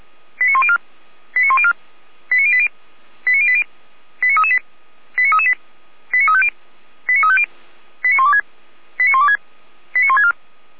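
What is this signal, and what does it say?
Five-tone selective call (ZVEI), heard as quick runs of five tones, tones of about 70 ms. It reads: 89104 89104 89080 89080 89338 89338 89350 89350 89126 89126 89164
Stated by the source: ZVEI five-tone call 89262 received on SDR, Feuerwehr Orlamünde.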